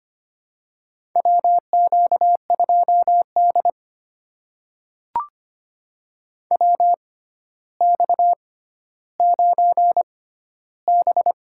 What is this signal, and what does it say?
Morse code, heard as a keyed tone, about 25 wpm, spaced as for 13 wpm, 700 Hz. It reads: WQ2D WX9B
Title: USA Call Signs - 4 Characters - 6x Character Spacing - 25wpm